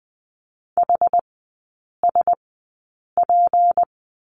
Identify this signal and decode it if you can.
Morse code, heard as a keyed tone, about 20 wpm, spaced as for 8 wpm, 690 Hz. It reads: HSP
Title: 159 Lesson 26e - Letter g - Three Characters 20wpm